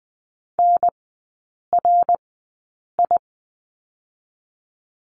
Morse code, heard as a keyed tone, about 20 wpm, spaced as for 8 wpm, 700 Hz. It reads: NRI
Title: 107 Lesson 18e - Letter m - Three Characters 20wpm